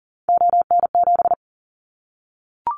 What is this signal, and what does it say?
Morse code, heard as a keyed tone, about 40 wpm, 700 Hz. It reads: ON7
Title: Callsign Prefixes for Top 50 DXCC - 40wpm